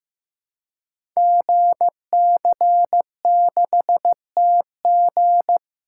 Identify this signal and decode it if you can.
Morse code, heard as a keyed tone, about 15 wpm, 700 Hz. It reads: GC6TG
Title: Code Groups - 5 Letter:Numbers - 15wpm